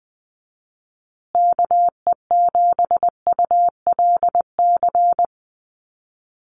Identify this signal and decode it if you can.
Morse code, heard as a keyed tone, about 20 wpm, 700 Hz. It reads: KE7ULC
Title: USA Call Signs Repeated 2x 20wpm